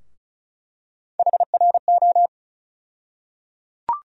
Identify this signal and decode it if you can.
Morse code, heard as a keyed tone, about 36 wpm, 700 Hz. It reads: HRO
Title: USA Call Sign Postfixes - 3 Letters 35wpm